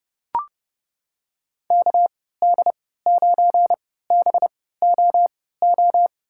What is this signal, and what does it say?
Morse code, heard as a keyed tone, about 30 wpm, spaced as for 17 wpm, 700 Hz. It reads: KD9BOO
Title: USA Call Signs - 6 Characters - 3x Character Spacing - 30wpm